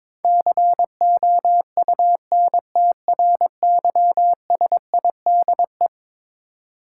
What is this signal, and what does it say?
Morse code, heard as a keyed tone, about 22 wpm, 700 Hz. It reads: COUNTRYSIDE
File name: practice / 22wpm / Eleven Letter Words - 22wpm